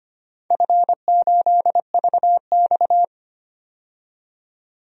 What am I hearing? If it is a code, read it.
Morse code, heard as a keyed tone, about 25 wpm, 700 Hz. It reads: F8VX
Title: DXCC-100 Call Signs - 4 Characters - 25wpm